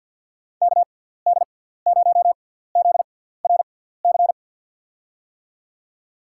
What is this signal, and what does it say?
Morse code, heard as a keyed tone, about 50 wpm, spaced as for 17 wpm, 700 Hz. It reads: KD0ZRC
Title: USA Call Signs - 6 Characters - 6x Character Spacing - 50wpm